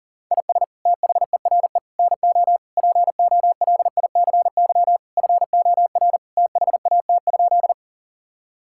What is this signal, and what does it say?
Morse code, heard as a keyed tone, about 37 wpm, 700 Hz. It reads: IS THERE NO POLICY FOR THAT?
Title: Sentences from Top 1000 Words - Review Part 5 - 40wpm